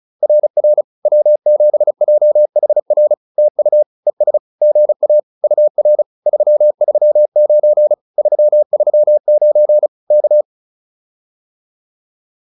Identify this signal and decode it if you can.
Morse code, heard as a keyed tone, about 34 wpm, 600 Hz. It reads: RR W7JHR TU ES GA UR 339 339 K
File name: POTA307 35